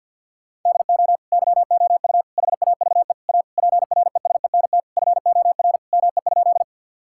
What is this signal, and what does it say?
Morse code, heard as a keyed tone, about 50 wpm, 700 Hz. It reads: DO YOU HAVE A PRESENT FOR ME?
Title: Sentences from Top 1000 Words - Review Part 5 - 50wpm